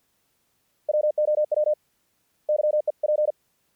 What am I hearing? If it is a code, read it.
Morse code, heard as a keyed tone, about 48 wpm, 600 Hz. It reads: WOW YEP
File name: archived / CRSnet